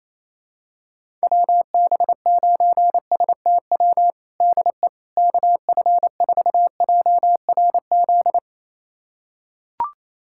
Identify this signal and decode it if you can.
Morse code, heard as a keyed tone, about 27 wpm, 700 Hz. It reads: WB9STW DE KF4JRZ
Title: USA Callsign DE Callsign 28wpm